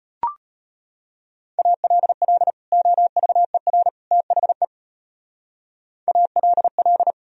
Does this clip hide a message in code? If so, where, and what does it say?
Yes: Morse code from 1.6 to 7.1 s, heard as a keyed tone, about 33 wpm, 700 Hz. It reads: ALL OVER THE ALL